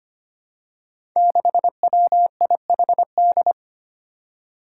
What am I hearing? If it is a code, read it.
Morse code, heard as a keyed tone, about 25 wpm, 700 Hz. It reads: 6WIHD
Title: Five Challenging Letter-Number Combinations 25wpm